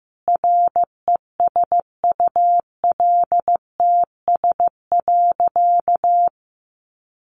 Morse code, heard as a keyed tone, about 15 wpm, 700 Hz. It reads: RESULTS.
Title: A Sound Of Thunder 15wpm